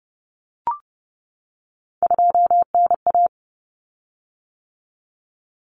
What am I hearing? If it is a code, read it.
Morse code, heard as a keyed tone, about 30 wpm, 700 Hz. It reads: 2NA
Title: Common Cut Numbers - 3-Digit 30wpm